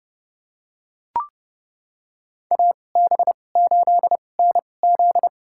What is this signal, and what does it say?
Morse code, heard as a keyed tone, about 30 wpm, spaced as for 22 wpm, 700 Hz. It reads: AB8NZ
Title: USA Call Signs - 5 Characters - 2x Character Spacing - 30wpm